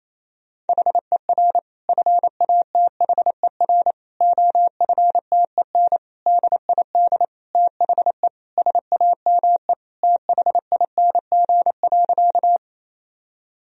Morse code, heard as a keyed tone, about 28 wpm, 700 Hz. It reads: HER FATHER OFTEN DID THE SAME THING.
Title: Sentences from Top 1000 Words - Review Part 1 - 28wpm